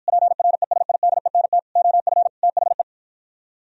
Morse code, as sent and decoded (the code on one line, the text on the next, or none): .--. .-. ... .. -.. -. - / --- ..-. / - .... .
PRESIDENT OF THE